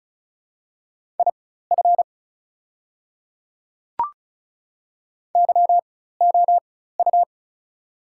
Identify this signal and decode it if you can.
Morse code, heard as a keyed tone, about 34 wpm, spaced as for 16 wpm, 690 Hz. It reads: IF YOU
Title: 4x Character Spacing - Top 300 Words - 35wpm